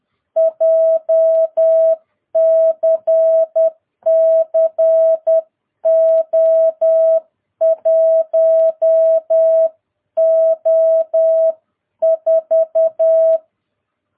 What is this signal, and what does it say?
Morse code, heard as a keyed tone, about 10 wpm, 660 Hz. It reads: JCC O1O4
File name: jh8cbh / 2022cw